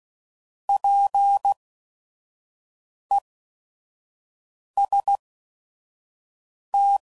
Morse code, heard as a keyed tone, about 16 wpm, spaced as for 5 wpm, 800 Hz. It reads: PEST
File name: Prefixes / By First Letter / Ps